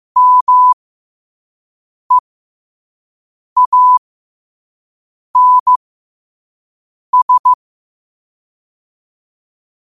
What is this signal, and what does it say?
Morse code, heard as a keyed tone, about 15 wpm, spaced as for 5 wpm, 1000 Hz. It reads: MEANS
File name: qotdfiles / 5wpm